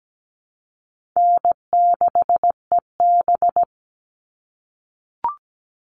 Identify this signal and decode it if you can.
Morse code, heard as a keyed tone, about 17 wpm, 700 Hz. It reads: N6EB
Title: All Cut Numbers - 4-Digit 17wpm